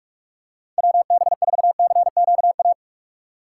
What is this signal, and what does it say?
Morse code, heard as a keyed tone, about 46 wpm, 700 Hz. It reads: WB4CQA